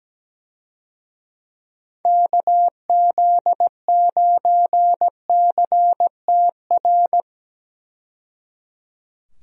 Morse code, heard as a keyed tone, about 17 wpm, 700 Hz. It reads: KZ9CTR
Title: USA Call Signs - 6 Characters - 17wpm